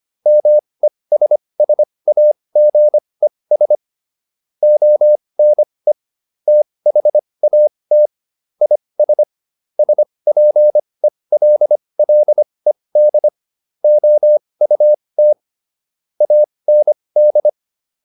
Morse code, heard as a keyed tone, about 25 wpm, spaced as for 20 wpm, 600 Hz. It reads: MESSAGES ONE THAT IS SPELLED OUT AND